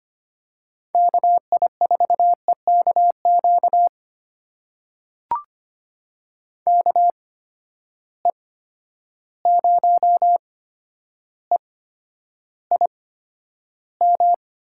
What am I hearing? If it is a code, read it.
Morse code, heard as a keyed tone, about 25 wpm, spaced as for 11 wpm, 700 Hz. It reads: KI4EKQ KE0EIM
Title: USA Call Signs - 6 Characters - 8x Character Spacing - 25wpm